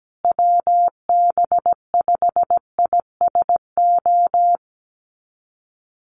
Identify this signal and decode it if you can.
Morse code, heard as a keyed tone, about 17 wpm, 700 Hz. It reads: WB5ISO
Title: USA Call Signs Repeated 3x 17wpm